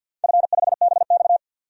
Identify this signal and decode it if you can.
Morse code, heard as a keyed tone, about 48 wpm, 700 Hz. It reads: F5BX
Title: DXCC-100 Call Signs - 4 Characters - 50wpm